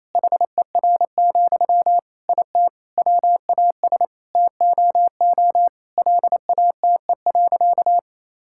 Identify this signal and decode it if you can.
Morse code, heard as a keyed tone, about 28 wpm, 700 Hz. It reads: HER, IT WAS TOO LATE.
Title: Sentences from Top 1000 Words - Review Part 6 - 28wpm